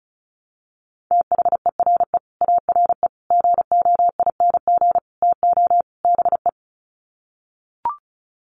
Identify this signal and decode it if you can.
Morse code, heard as a keyed tone, about 34 wpm, 700 Hz. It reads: THERE ARE GOING TO BE
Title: Sets of 5 Words 35wpm